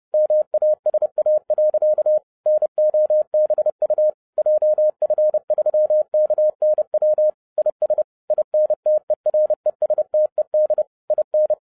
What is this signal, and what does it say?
Morse code, heard as a keyed tone, about 30 wpm, 610 Hz. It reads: MASA. NOBU JF3KNW IS INTERESTED IN